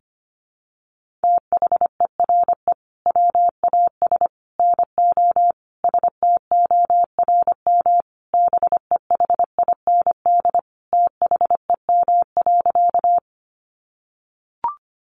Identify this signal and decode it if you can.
Morse code, heard as a keyed tone, about 25 wpm, 700 Hz. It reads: THERE WAS NO STORM BEHIND THEM.
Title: Sentences from Top 1500 Words 25wpm